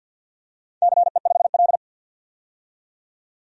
Morse code, heard as a keyed tone, about 53 wpm, 700 Hz. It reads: CE5L
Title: DXCC-100 Call Signs - 4 Characters - 50wpm